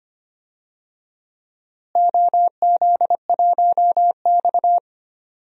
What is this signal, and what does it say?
Morse code, heard as a keyed tone, about 25 wpm, 700 Hz. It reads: OZ1X
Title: DXCC-100 Call Signs - 4 Characters - Repeated 2x - 25wpm